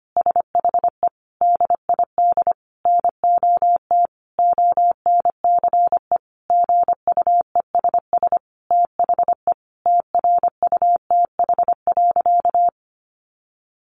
Morse code, heard as a keyed tone, about 25 wpm, 700 Hz. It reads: SHE DID NOT ONCE GUESS THE TRUTH.
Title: Sentences from Top 1500 Words 25wpm